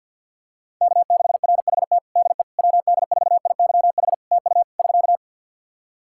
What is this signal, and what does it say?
Morse code, heard as a keyed tone, about 50 wpm, 700 Hz. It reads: K6RHT DE WD4IYH TU <SK>
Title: Practice QSO - Exchange 4 - Ending - 50wpm